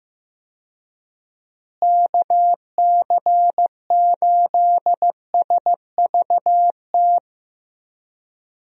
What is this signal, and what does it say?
Morse code, heard as a keyed tone, about 15 wpm, 700 Hz. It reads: KC8SVT
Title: USA Call Signs - 6 Characters - Repeated 3x - 15wpm